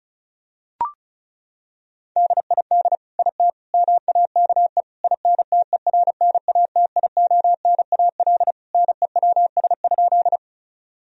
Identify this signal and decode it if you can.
Morse code, heard as a keyed tone, about 36 wpm, 700 Hz. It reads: DID IT MAKE INTERNATIONAL NEWS?
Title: Sentences from Top 1000 Words - Review Part 4 - 35wpm